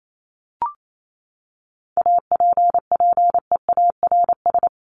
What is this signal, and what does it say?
Morse code, heard as a keyed tone, about 28 wpm, 700 Hz. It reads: APPEARS
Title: ES:S Suffix - Top 100 Words with Base Words 28wpm